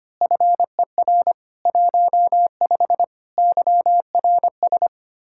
Morse code, heard as a keyed tone, about 25 wpm, 700 Hz. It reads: FER 15 YRS
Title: Speed-Racing Common QSO Phrases Encore - 25wpm